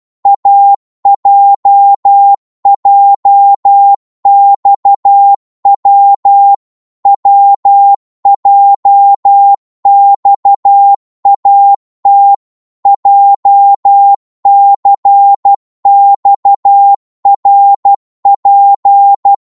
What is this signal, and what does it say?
Morse code, heard as a keyed tone, about 12 wpm, 800 Hz. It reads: AJJXW WJXAT JCXRP
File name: Morse / ltr-12wpm